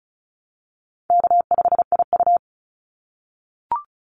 Morse code, heard as a keyed tone, about 36 wpm, 700 Hz. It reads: K5IU
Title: Call Signs - 35wpm